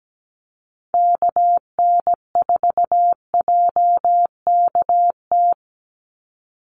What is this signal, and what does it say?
Morse code, heard as a keyed tone, about 17 wpm, 700 Hz. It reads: KN4JKT
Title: USA Call Signs 17wpm